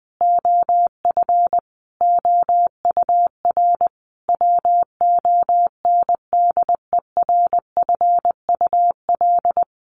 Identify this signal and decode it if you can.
Morse code, heard as a keyed tone, about 20 wpm, 700 Hz. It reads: OF OUR WONDERFUL
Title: Sentences from Top 1100 Words 20wpm